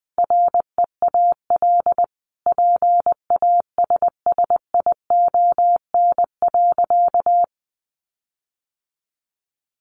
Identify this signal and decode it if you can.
Morse code, heard as a keyed tone, about 20 wpm, 700 Hz. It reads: REAL PASSION.
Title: Sentences from Top 1300 Words 20wpm